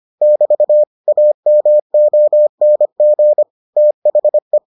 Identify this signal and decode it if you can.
Morse code, heard as a keyed tone, about 25 wpm, 600 Hz. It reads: <BT> AMONG THE